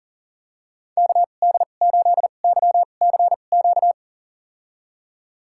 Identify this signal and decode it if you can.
Morse code, heard as a keyed tone, about 38 wpm, spaced as for 28 wpm, 700 Hz. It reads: KD8YCQ
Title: USA Call Signs - 2x Character Spacing - 40wpm